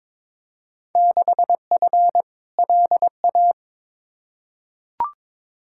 Morse code, heard as a keyed tone, about 22 wpm, 700 Hz. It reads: 6F LA
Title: Field Day Exchanges Encore - 22wpm